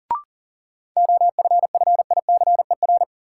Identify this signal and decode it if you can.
Morse code, heard as a keyed tone, about 40 wpm, 700 Hz. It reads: OFFICER
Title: Sentences from Top 1200 Words - New Word Drill - 40wpm